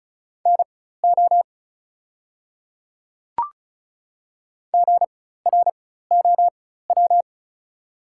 Morse code, heard as a keyed tone, about 36 wpm, spaced as for 16 wpm, 700 Hz. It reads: NO GROW
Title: 4x Character Spacing - Top 500 Words - 35wpm